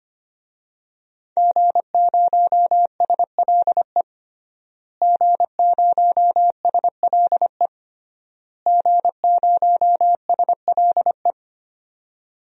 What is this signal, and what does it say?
Morse code, heard as a keyed tone, about 25 wpm, 700 Hz. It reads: G0SLE G0SLE G0SLE
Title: DXCC-100 Call Signs - 5 Characters - Repeated 3x - 25wpm